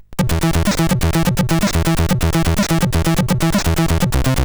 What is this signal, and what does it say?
There is no code.